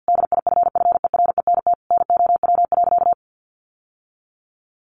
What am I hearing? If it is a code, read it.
Morse code, heard as a keyed tone, about 50 wpm, 700 Hz. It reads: DIFFERENT NOW.
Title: Sentences from Top 1200 Words 50wpm